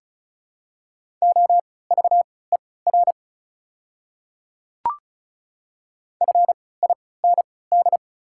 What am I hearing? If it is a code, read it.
Morse code, heard as a keyed tone, about 36 wpm, spaced as for 20 wpm, 690 Hz. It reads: OVER FIND
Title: 3x Character Spacing - Top 100 Words - 35wpm